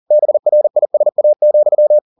Morse code, heard as a keyed tone, about 40 wpm, 600 Hz. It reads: BRISA,